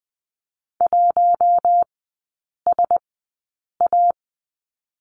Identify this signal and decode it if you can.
Morse code, heard as a keyed tone, about 20 wpm, spaced as for 8 wpm, 700 Hz. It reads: 1SA
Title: 047 Lesson 9e - Number 4 - Three Characters 20wpm